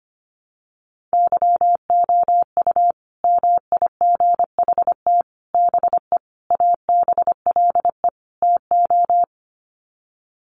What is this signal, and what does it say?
Morse code, heard as a keyed tone, about 25 wpm, 700 Hz. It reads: YOU MIGHT BE ABLE TO
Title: Sets of 5 Words Encore2 25wpm